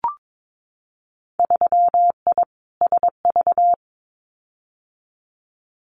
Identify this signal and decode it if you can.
Morse code, heard as a keyed tone, about 22 wpm, 700 Hz. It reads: 3I SV
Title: Winter Field Day Exchanges Encore 22wpm